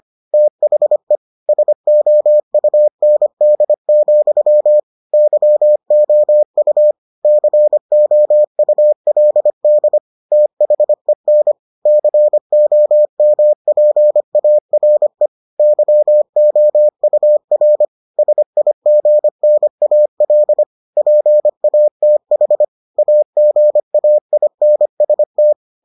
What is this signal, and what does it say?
Morse code, heard as a keyed tone, about 25 wpm, 600 Hz. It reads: THE SOUND, YOU COULD THEN COMPARE YOUR SIGNAL PATH AGAINST